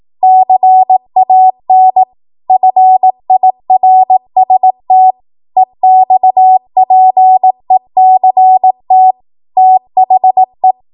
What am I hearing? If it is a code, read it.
Morse code, heard as a keyed tone, about 18 wpm, 750 Hz. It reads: CAN FIRST EXPECT THE
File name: Morse / 18WPM